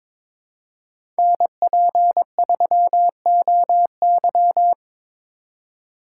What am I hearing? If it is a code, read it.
Morse code, heard as a keyed tone, about 23 wpm, 700 Hz. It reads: NP3OY
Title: USA Call Signs - 5 Characters - 4x Character Spacing - 22wpm